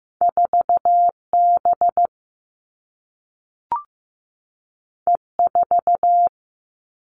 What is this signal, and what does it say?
Morse code, heard as a keyed tone, about 15 wpm, 700 Hz. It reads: 4 B E 4
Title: Two Challenging Letter-Number Combinations 15wpm